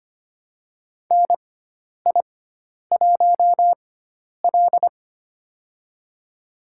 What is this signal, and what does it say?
Morse code, heard as a keyed tone, about 25 wpm, spaced as for 10 wpm, 700 Hz. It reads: NI1L
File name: Lesson 16g - Letter c - Call Signs 25wpm